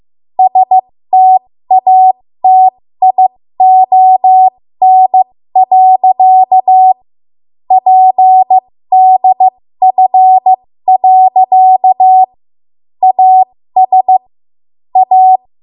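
Morse code, heard as a keyed tone, about 15 wpm, spaced as for 13 wpm, 750 Hz. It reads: STATION. PDF. AS A
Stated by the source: ARRL Morse code practice at 13 WPM.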